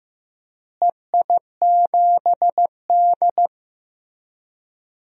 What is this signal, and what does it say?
Morse code, heard as a keyed tone, about 15 wpm, 700 Hz. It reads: EI7D
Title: DXCC-100 Call Signs - 4 Characters - Repeated 2x - 15wpm